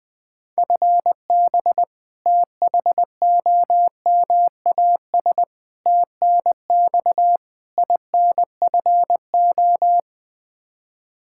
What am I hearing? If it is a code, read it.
Morse code, heard as a keyed tone, about 20 wpm, 700 Hz. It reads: FB THOMAS TNX INFO
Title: Practice QSO - Exchange 3 - More Info - 20wpm